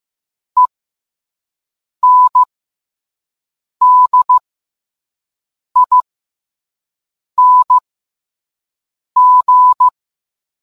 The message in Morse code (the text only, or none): ENDING